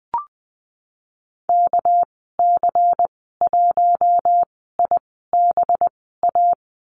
Morse code, heard as a keyed tone, about 20 wpm, spaced as for 15 wpm, 700 Hz. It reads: KC1IBA
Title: 245 Lesson 39g - Letter z - Call Signs 20wpm